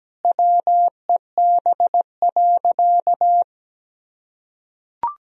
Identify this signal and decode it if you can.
Morse code, heard as a keyed tone, about 17 wpm, 700 Hz. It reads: WEB.
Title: Sentences from Top 1000 Words - Review Part 4 - 17wpm